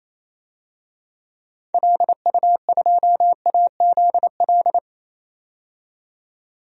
Morse code, heard as a keyed tone, about 28 wpm, 700 Hz. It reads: LU2AZL
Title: DXCC-100 Call Signs - 6 Characters - 28wpm